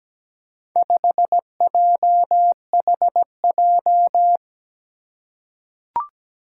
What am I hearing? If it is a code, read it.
Morse code, heard as a keyed tone, about 17 wpm, 700 Hz. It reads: 5JHJ
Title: Four Challenging Letter-Number Combinations 17wpm